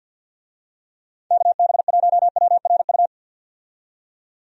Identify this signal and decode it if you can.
Morse code, heard as a keyed tone, about 51 wpm, 700 Hz. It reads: KB1WRU
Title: USA Call Signs - 6 Characters - 50wpm